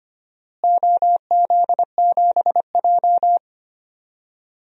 Morse code, heard as a keyed tone, about 25 wpm, 700 Hz. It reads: OZ7J
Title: DXCC-100 Call Signs - 4 Characters - Repeated 2x - 25wpm